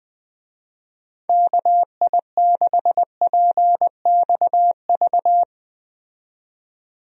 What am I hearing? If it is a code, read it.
Morse code, heard as a keyed tone, about 20 wpm, 700 Hz. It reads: KI6PXV
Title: USA Call Signs - 6 Characters - 20wpm